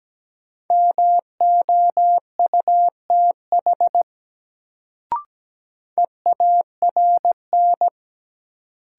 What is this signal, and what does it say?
Morse code, heard as a keyed tone, about 17 wpm, 700 Hz. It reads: MOUTH EARN
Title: Sentences from Top 1500 Words - New Word Drill - 17wpm